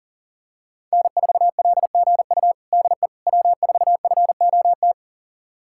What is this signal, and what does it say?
Morse code, heard as a keyed tone, about 40 wpm, 700 Hz. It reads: N4LGU DE W4FOT
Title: USA Callsign DE Callsign - 5 Characters - 40wpm